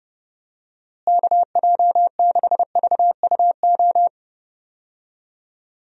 Morse code, heard as a keyed tone, about 30 wpm, 700 Hz. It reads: KJ6VUO
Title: USA Call Signs - 6 Characters - Repeated 3x - 30wpm